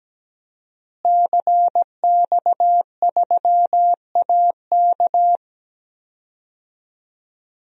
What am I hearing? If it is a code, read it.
Morse code, heard as a keyed tone, about 17 wpm, 700 Hz. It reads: CX3AK
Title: DXCC-100 Call Signs - 5 Characters - Repeated 2x - 17wpm